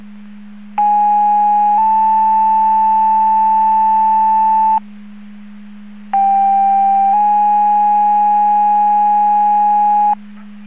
Two-tone page, heard as a one-second tone, then a three-second tone, about 830 then 880 Hz, about 790 then 830 Hz. It is heard from 0.8 to 10.1 s.